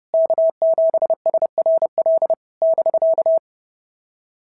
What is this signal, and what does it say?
Morse code, heard as a keyed tone, about 30 wpm, 650 Hz. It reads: K7SRL <BK>
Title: SOTA QSO Practice 30wpm